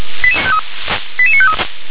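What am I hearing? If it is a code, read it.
Five-tone selective call (ZVEI), heard as quick runs of five tones, tones of about 70 ms. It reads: 88953
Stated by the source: ZVEI five-tone call 88920, received on SDR.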